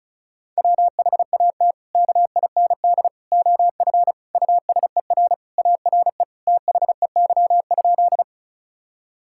Morse code, heard as a keyed tone, about 36 wpm, 700 Hz. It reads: WHAT KIND OF USER ARE THEY?
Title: Sentences from Top 1000 Words - Review Part 5 - 35wpm